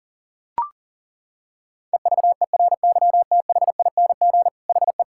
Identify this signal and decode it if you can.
Morse code, heard as a keyed tone, about 37 wpm, 700 Hz. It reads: EVERYTHING HE
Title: Sentences from Top 1000 Words - Review Part 6 - 40wpm